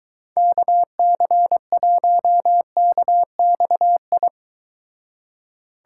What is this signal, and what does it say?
Morse code, heard as a keyed tone, about 23 wpm, 700 Hz. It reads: KC1KXI